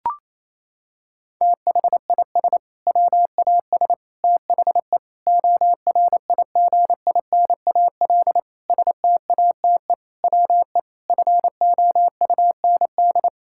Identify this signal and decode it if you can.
Morse code, heard as a keyed tone, about 28 wpm, 700 Hz. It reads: THIS WAS THE ORIGINAL STATE WE FOUND
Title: Sentences from Top 1000 Words - Review Part 6 - 28wpm